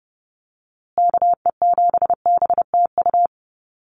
Morse code, heard as a keyed tone, about 30 wpm, 700 Hz. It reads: KE7BTU
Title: USA Call Signs Repeated 2x 30wpm